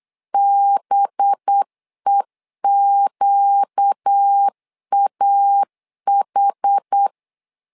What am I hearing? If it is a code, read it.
Morse code, heard as a keyed tone, about 8 wpm, 800 Hz. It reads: BEQAH
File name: LEZIONI / 5 LBVC / 5 LB 8wpm 2816